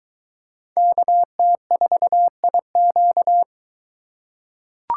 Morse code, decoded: KT4IQ

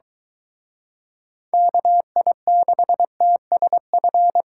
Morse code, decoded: KI6TSF

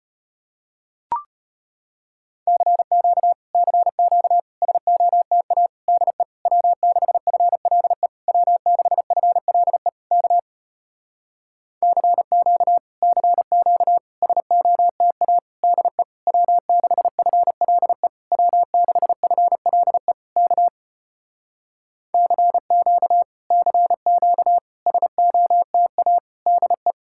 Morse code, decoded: CQ CQ SOTA DE W6FLE W6FLE K CQ CQ SOTA DE W6FLE W6FLE K CQ CQ SOTA DE

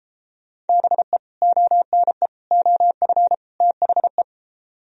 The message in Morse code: BE ONE OF THE

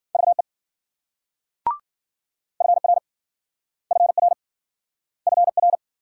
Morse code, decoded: VE FR FR FR

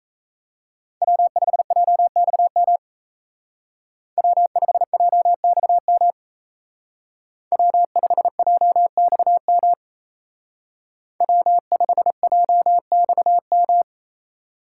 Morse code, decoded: W5JXM W5JXM W5JXM W5JXM

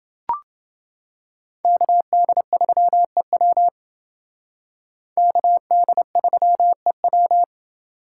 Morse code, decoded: KD3EW KD3EW

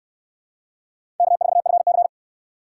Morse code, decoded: K4FC

